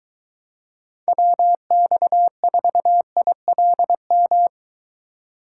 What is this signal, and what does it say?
Morse code, heard as a keyed tone, about 23 wpm, 700 Hz. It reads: WX4ILM